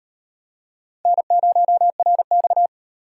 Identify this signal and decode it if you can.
Morse code, heard as a keyed tone, about 38 wpm, 700 Hz. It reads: N0RX